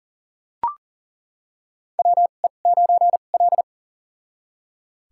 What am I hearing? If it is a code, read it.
Morse code, heard as a keyed tone, about 40 wpm, spaced as for 29 wpm, 700 Hz. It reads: WE9L
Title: USA Call Signs - 4 Characters - 2x Character Spacing - 40wpm